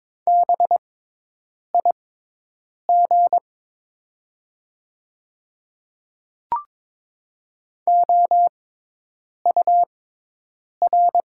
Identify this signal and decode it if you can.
Morse code, heard as a keyed tone, about 22 wpm, spaced as for 8 wpm, 700 Hz. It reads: BIG OUR
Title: 6x Character Spacing - Top 300 Words - 22wpm